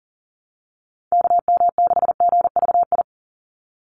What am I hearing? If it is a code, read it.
Morse code, heard as a keyed tone, about 37 wpm, 700 Hz. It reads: KM6GVI
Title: USA Call Signs 40wpm